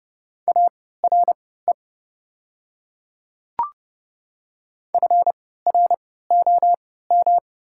Morse code, heard as a keyed tone, about 30 wpm, spaced as for 17 wpm, 700 Hz. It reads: ARE FROM
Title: 3x Character Spacing - Top 100 Words - 30wpm